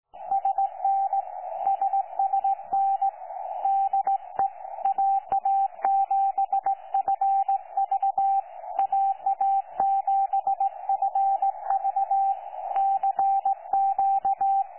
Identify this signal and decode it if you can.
Morse code, heard as a keyed tone, about 18 wpm, 780 Hz, with radio noise. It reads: SN SSN DE AA7FV AA7FV CQ
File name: 12 45Z 18088.1kHz